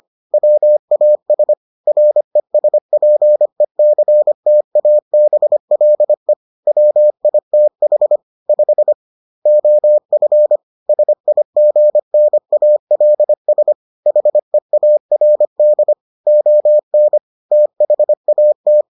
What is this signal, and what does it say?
Morse code, heard as a keyed tone, about 25 wpm, 600 Hz. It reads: WAS RESPECTABLE WITH 5 OF SIGNALS HEARD ON THAT